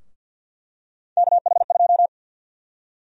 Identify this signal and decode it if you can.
Morse code, heard as a keyed tone, about 52 wpm, 700 Hz. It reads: KH2
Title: USA Call Sign Prefixes - Repeated 2x - 50wpm